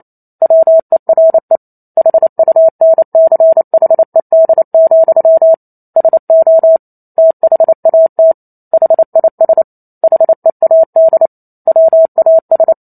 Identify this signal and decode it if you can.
Morse code, heard as a keyed tone, about 28 wpm, 650 Hz. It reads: WERE HUNCHED, SO THAT HIS HEAD WAS